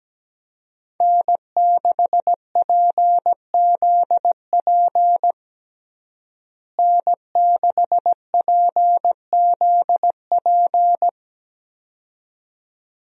Morse code, heard as a keyed tone, about 17 wpm, 700 Hz. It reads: N6PZP N6PZP
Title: USA Call Signs - 5 Characters - Repeated 2x - 17wpm